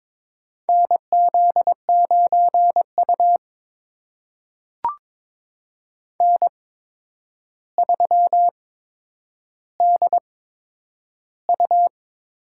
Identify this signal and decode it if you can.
Morse code, heard as a keyed tone, about 22 wpm, spaced as for 9 wpm, 700 Hz. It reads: NZ9U N3DU
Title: USA Call Signs - 4 Characters - 8x Character Spacing - 22wpm